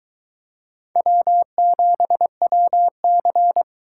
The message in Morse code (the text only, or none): W7WC